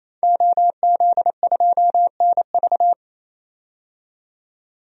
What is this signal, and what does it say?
Morse code, heard as a keyed tone, about 28 wpm, 700 Hz. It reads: OZ2NV
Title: DXCC-100 Call Signs - 5 Characters - 28wpm